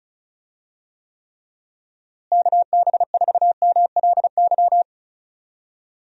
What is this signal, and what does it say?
Morse code, heard as a keyed tone, about 36 wpm, 700 Hz. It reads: KB4MLY
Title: USA Call Signs - 6 Characters - Repeated 2x - 35wpm